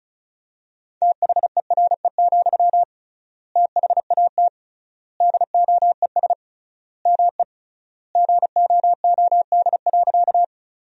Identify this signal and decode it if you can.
Morse code, heard as a keyed tone, about 36 wpm, 700 Hz. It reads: THERE, THAT DOES ME GOOD.
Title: 3x Word-Spacing - Sentences from Top 200 Words - 35wpm